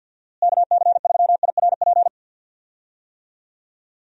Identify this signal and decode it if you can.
Morse code, heard as a keyed tone, about 46 wpm, 700 Hz. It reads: KC3IRP
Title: USA Call Signs - 6 Characters - Repeated 3x - 50wpm